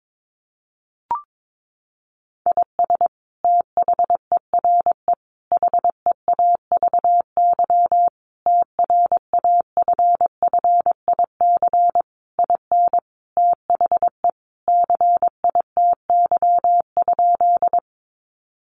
Morse code, heard as a keyed tone, about 22 wpm, 700 Hz. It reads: IS THERE HEAVY TRAFFIC IN THE CITY?
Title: Sentences from Top 1300 Words 22wpm